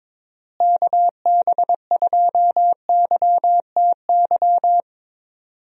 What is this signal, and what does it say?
Morse code, heard as a keyed tone, about 22 wpm, 700 Hz. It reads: KB2YTY
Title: USA Call Signs - 6 Characters - Repeated 3x - 22wpm